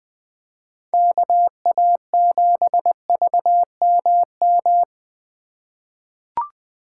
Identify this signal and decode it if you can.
Morse code, heard as a keyed tone, about 20 wpm, 700 Hz. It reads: KA7VMM
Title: USA Call Signs - 6 Characters - 2x Character Spacing - 20wpm